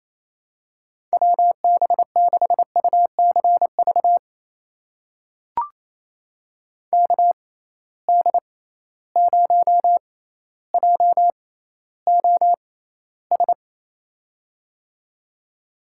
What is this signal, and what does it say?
Morse code, heard as a keyed tone, about 28 wpm, spaced as for 14 wpm, 700 Hz. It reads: WB6UCV KD0JOS
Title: USA Call Signs - 6 Characters - 6x Character Spacing - 28wpm